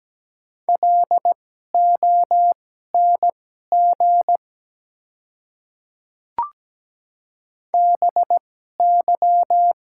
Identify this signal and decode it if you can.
Morse code, heard as a keyed tone, about 17 wpm, spaced as for 12 wpm, 700 Hz. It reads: LONG BY
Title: 2x Character Spacing - Top 100 Words - 17wpm